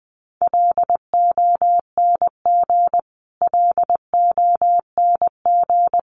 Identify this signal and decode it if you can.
Morse code, heard as a keyed tone, about 20 wpm, 700 Hz. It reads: LONG LONG
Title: Top 300 Words Repeated 20wpm